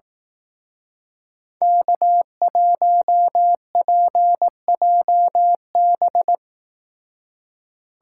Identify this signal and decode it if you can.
Morse code, heard as a keyed tone, about 18 wpm, 700 Hz. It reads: K1PJB
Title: Speed Racing - USA 1x3 Call Signs 15wpm